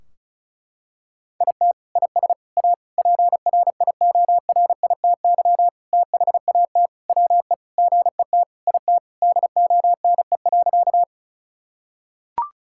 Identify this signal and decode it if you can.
Morse code, heard as a keyed tone, about 34 wpm, 700 Hz. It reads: IT IS A PRIORITY THAT WE GET IT DONE.
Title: Sentences from Top 1600 Words 35wpm